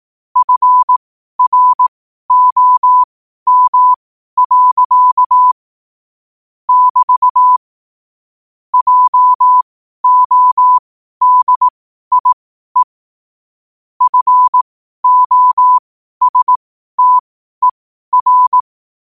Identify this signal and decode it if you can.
Morse code, heard as a keyed tone, about 18 wpm, spaced as for 12 wpm, 1000 Hz. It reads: FROM. <BT> JODIE FOSTER
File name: qotdfiles / 12wpm